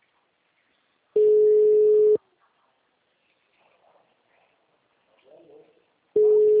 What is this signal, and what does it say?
Ringback tone (Europe, 425 Hz), with speech.